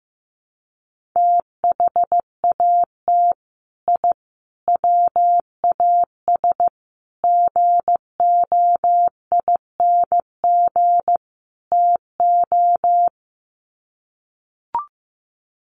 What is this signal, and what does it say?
Morse code, heard as a keyed tone, about 15 wpm, 700 Hz. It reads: THAT I WAS GOING TO
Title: Sets of 5 Words Encore 15wpm